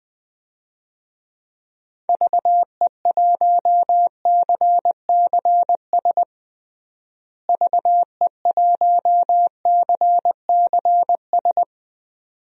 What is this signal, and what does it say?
Morse code, heard as a keyed tone, about 20 wpm, 700 Hz. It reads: VE1CCS VE1CCS
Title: DXCC-100 Call Signs - 6 Characters - Repeated 2x - 20wpm